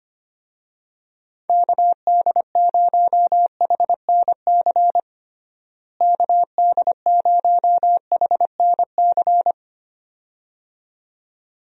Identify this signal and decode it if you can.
Morse code, heard as a keyed tone, about 25 wpm, 700 Hz. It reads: KD0HNC KD0HNC
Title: USA Call Signs - 6 Characters - Repeated 2x - 25wpm